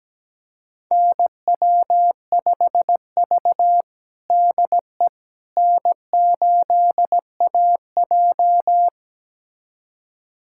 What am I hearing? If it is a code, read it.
Morse code, heard as a keyed tone, about 17 wpm, 700 Hz. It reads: NW5V DE N8AJ
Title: USA Callsign DE Callsign - 4 Characters - 17wpm